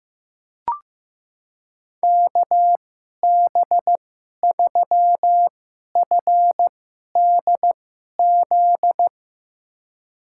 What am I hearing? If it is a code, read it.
Morse code, heard as a keyed tone, about 15 wpm, spaced as for 11 wpm, 700 Hz. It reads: KB3FDZ
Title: USA Call Signs - 6 Characters - 2x Character Spacing - 15wpm